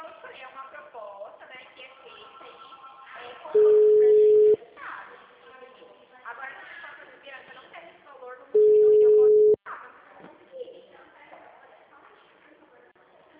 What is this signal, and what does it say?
Ringback tone (Europe, 425 Hz), with speech.